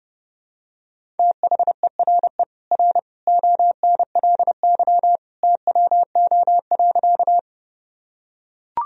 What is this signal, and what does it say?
Morse code, heard as a keyed tone, about 30 wpm, 700 Hz. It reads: THERE R ONLY TWO.
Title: Sentences from Top 200 Words and Abbreviations 30wpm